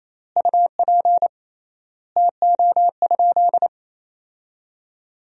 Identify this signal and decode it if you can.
Morse code, heard as a keyed tone, about 28 wpm, 700 Hz. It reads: UP TO?
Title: 3x Word-Spacing - Sentences from Top 100 Words - 28wpm